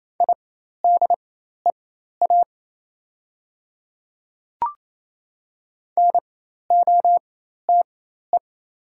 Morse code, heard as a keyed tone, about 28 wpm, spaced as for 13 wpm, 700 Hz. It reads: IDEA NOTE
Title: 4x Character Spacing - Top 500 Words - 28wpm